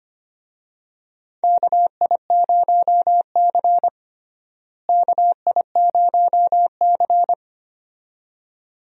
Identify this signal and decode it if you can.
Morse code, heard as a keyed tone, about 25 wpm, 700 Hz. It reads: KI0C KI0C
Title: USA Call Signs - 4 Characters - Repeated 2x - 25wpm